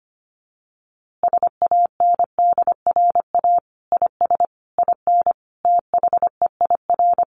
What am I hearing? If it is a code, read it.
Morse code, heard as a keyed tone, about 25 wpm, 700 Hz. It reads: SANDRA IS IN THEIR
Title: Easy Phrases 25wpm